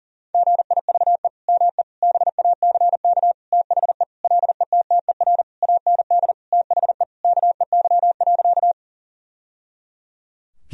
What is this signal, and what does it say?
Morse code, heard as a keyed tone, about 40 wpm, 700 Hz. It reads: GIVE ME BACK THE LETTER AND THE KEY.
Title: Sentences from Top 1000 Words - Review Part 4 - 40wpm